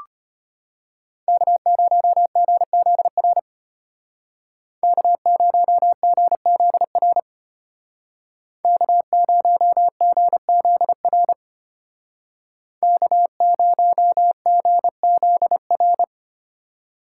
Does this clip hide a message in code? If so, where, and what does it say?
Yes: Morse code from 1.3 to 16.0 s, heard as a keyed tone, about 32 wpm, 700 Hz. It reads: K0GZR K0GZR K0GZR K0GZR